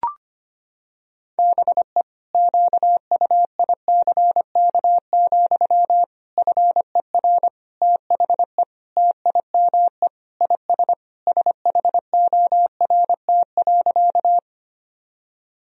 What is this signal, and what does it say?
Morse code, heard as a keyed tone, about 25 wpm, 700 Hz. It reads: BE QUICK, FER THE TIME IS SHORT.
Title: Sentences from Top 500 Words and Abbreviations 25wpm